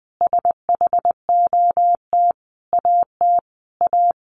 Morse code, heard as a keyed tone, about 20 wpm, 700 Hz. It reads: SHOT AT A
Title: Sentences from Top 1100 Words 20wpm